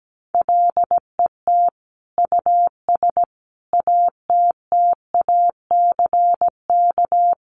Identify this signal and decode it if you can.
Morse code, heard as a keyed tone, about 17 wpm, 700 Hz. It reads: LET US ATTACK
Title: Sentences from Top 1500 Words 17wpm